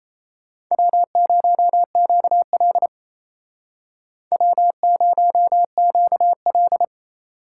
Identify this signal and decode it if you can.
Morse code, heard as a keyed tone, about 37 wpm, 700 Hz. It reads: W0QL W0QL